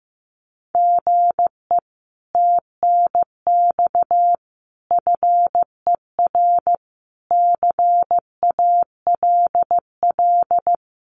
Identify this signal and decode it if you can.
Morse code, heard as a keyed tone, about 15 wpm, 700 Hz. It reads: GE TNX FER CALL
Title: Common QSO Phrases Encore 15wpm